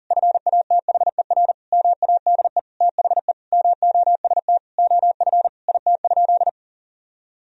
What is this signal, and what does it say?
Morse code, heard as a keyed tone, about 40 wpm, 700 Hz. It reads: FATHER MADE THE MOST OF IT?
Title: Sentences from Top 1000 Words - Review Part 1 - 40wpm